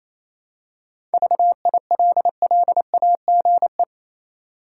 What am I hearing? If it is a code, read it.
Morse code, heard as a keyed tone, about 28 wpm, 700 Hz. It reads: VILLAGE
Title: Seven Letter Words - 28wpm